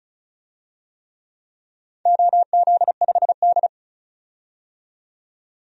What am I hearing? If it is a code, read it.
Morse code, heard as a keyed tone, about 35 wpm, 700 Hz. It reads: OZ5D
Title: DXCC-100 Call Signs - 4 Characters - 35wpm